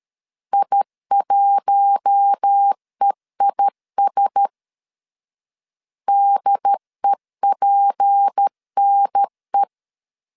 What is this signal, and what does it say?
Morse code, heard as a keyed tone, about 13 wpm, 800 Hz. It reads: I1EIS DEPNE
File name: CW / DET / 1 6 1500C S B 12wpm 3431